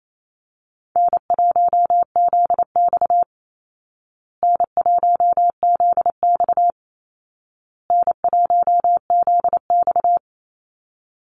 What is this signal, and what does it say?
Morse code, heard as a keyed tone, about 28 wpm, 700 Hz. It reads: N1ZX N1ZX N1ZX
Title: Easy Call Signs Repeated 3x - 28wpm